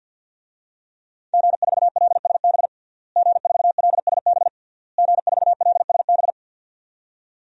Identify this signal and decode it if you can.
Morse code, heard as a keyed tone, about 52 wpm, 700 Hz. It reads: G4LSB G4LSB G4LSB
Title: DXCC-100 Call Signs - 5 Characters - Repeated 3x - 50wpm